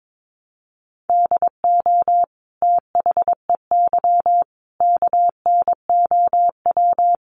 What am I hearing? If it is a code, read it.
Morse code, heard as a keyed tone, about 22 wpm, 700 Hz. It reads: DO THEY KNOW
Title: Sentences from Top 100 Words 22wpm